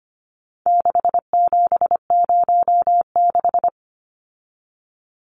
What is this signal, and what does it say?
Morse code, heard as a keyed tone, about 25 wpm, 700 Hz. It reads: 6706
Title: Numbers - 4-Digit 25wpm